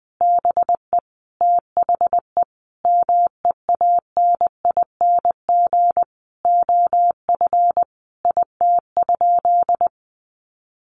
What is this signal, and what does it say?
Morse code, heard as a keyed tone, about 20 wpm, 700 Hz. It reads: BE THE MEANING OF IT?
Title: Sentences from Top 900 Words 20wpm